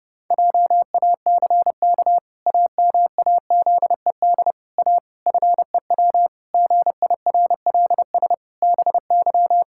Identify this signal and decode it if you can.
Morse code, heard as a keyed tone, about 30 wpm, 700 Hz. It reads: JACK AMAZED A FEW GIRLS BY
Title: Pangram Challenge v2 - 30wpm